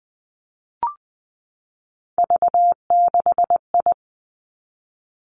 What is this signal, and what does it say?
Morse code, heard as a keyed tone, about 20 wpm, 700 Hz. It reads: V6I
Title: Three Challenging Letter-Number Combinations 20wpm